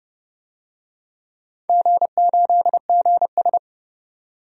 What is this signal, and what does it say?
Morse code, heard as a keyed tone, about 30 wpm, 700 Hz. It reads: G8GS
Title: DXCC-100 Call Signs - 4 Characters - 30wpm